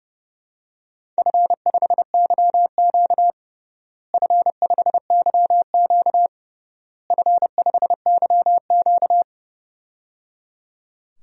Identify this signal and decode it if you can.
Morse code, heard as a keyed tone, about 30 wpm, 700 Hz. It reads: F5YQ F5YQ F5YQ
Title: DXCC-100 Call Signs - 4 Characters - Repeated 3x - 30wpm